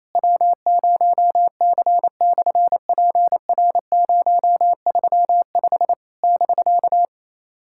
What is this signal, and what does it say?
Morse code, heard as a keyed tone, about 28 wpm, 700 Hz. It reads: W0C/PR035 <BK>
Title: SOTA QSO Practice 28wpm